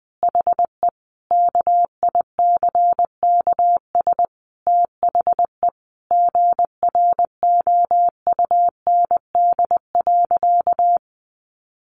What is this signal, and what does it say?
Morse code, heard as a keyed tone, about 20 wpm, 700 Hz. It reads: HE KICKS THE GROUND.
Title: Additional Easy Phrases 20wpm